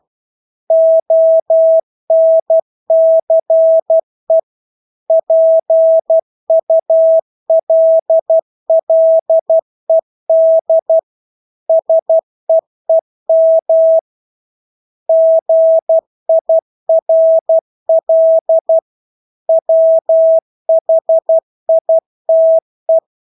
Morse code, heard as a keyed tone, about 12 wpm, 650 Hz. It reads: ONCE PULLED SEEM GIRL WHITE
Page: Engelske ord 12 wpm